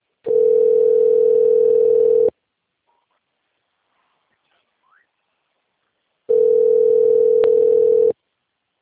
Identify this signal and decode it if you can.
Ringback tone (North America), 440 + 480 Hz.